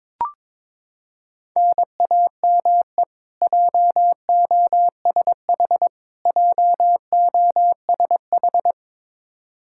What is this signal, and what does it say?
Morse code, heard as a keyed tone, about 22 wpm, 700 Hz. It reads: NAME JOSH JOSH
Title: Practice QSO - Exchange 1 - Beginning - 22wpm